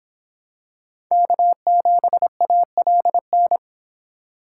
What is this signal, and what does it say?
Morse code, heard as a keyed tone, about 26 wpm, 700 Hz. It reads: K7ALN